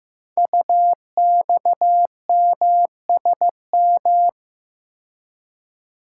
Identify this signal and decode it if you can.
Morse code, heard as a keyed tone, about 15 wpm, 700 Hz. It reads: UXMSM
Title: Code Groups - 5 Letters - 15wpm